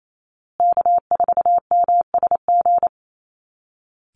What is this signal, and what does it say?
Morse code, heard as a keyed tone, about 28 wpm, 700 Hz. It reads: K4MSG
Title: Call Signs Repeated 2x - 28wpm